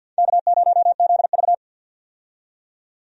Morse code, heard as a keyed tone, about 53 wpm, 700 Hz. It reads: K0ZV
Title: USA Call Signs - 4 Characters - 50wpm